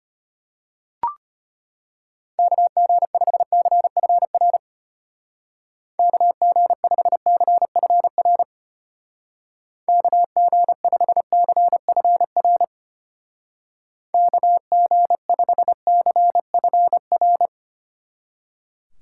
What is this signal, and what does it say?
Morse code, heard as a keyed tone, about 31 wpm, 700 Hz. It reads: KG5CFR KG5CFR KG5CFR KG5CFR
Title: Speed Racing - USA 2x3 Call Signs 25wpm